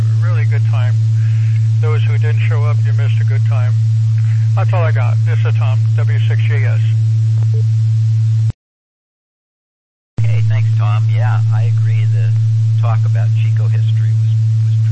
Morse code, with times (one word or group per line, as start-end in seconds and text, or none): none